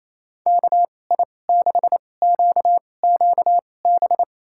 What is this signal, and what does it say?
Morse code, heard as a keyed tone, about 27 wpm, spaced as for 20 wpm, 700 Hz. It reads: KI6QQB
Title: USA Call Signs - 6 Characters - 2x Character Spacing - 28wpm